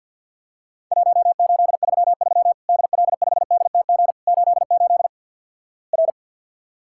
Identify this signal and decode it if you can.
Morse code, heard as a keyed tone, about 50 wpm, 700 Hz. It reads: 1833 DL5DTG 88 R
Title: Worked All Europe DX Contest - QTC Exchange - 50wpm